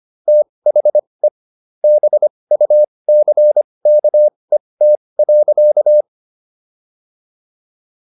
Morse code, heard as a keyed tone, about 25 wpm, spaced as for 20 wpm, 600 Hz. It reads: THE BUCKET.